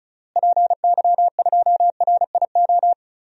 Morse code, heard as a keyed tone, about 34 wpm, 700 Hz. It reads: PY2RIO
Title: DXCC-100 Call Signs - 6 Characters - Repeated 3x - 35wpm